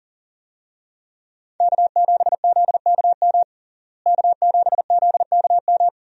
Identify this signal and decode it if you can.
Morse code, heard as a keyed tone, about 40 wpm, 700 Hz. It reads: K7ZKM K7ZKM
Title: USA Call Signs - 5 Characters - Repeated 2x - 40wpm